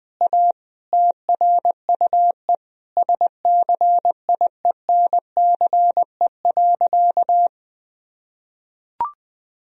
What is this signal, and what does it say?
Morse code, heard as a keyed tone, about 20 wpm, 700 Hz. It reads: A TRUE SCIENCE.
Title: Sentences from Top 1000 Words - Review Part 4 - 20wpm